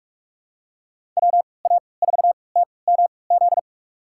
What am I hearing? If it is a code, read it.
Morse code, heard as a keyed tone, about 46 wpm, spaced as for 25 wpm, 700 Hz. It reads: WA4TMZ
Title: USA Call Signs - 3x Character Spacing - 45wpm